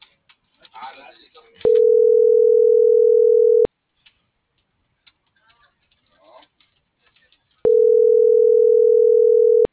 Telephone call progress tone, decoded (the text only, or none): ringback tone